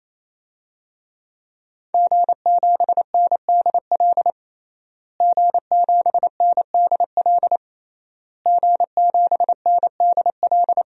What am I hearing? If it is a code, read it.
Morse code, heard as a keyed tone, about 27 wpm, 700 Hz. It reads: G7NDL G7NDL G7NDL
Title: DXCC-100 Call Signs - 5 Characters - Repeated 3x - 28wpm